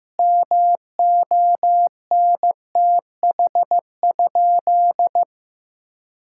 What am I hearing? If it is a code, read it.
Morse code, heard as a keyed tone, about 16 wpm, 700 Hz. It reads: MONTH?
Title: Sentences from Top 1000 Words - Review Part 6 - 15wpm